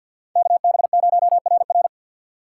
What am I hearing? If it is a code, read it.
Morse code, heard as a keyed tone, about 48 wpm, 700 Hz. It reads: KB0RR
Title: USA Call Signs - 5 Characters - 4x Character Spacing - 50wpm